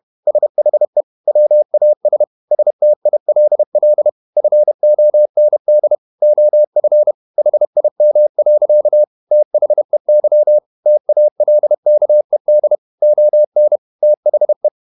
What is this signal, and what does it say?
Morse code, heard as a keyed tone, about 31 wpm, 600 Hz. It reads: SHE WAS STILL FOND OF HIM. THEY TALKED ON THE
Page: Never 31wpm | CW med Gnister